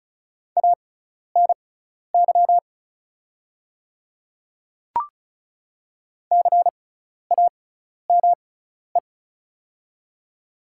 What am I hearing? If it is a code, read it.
Morse code, heard as a keyed tone, about 36 wpm, spaced as for 12 wpm, 700 Hz. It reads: ANY CAME
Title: 6x Character Spacing - Top 200 Words - 35wpm